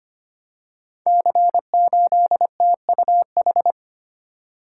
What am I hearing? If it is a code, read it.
Morse code, heard as a keyed tone, about 25 wpm, 700 Hz. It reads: C8TUH
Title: Code Groups - 5 Letter:Numbers - 25wpm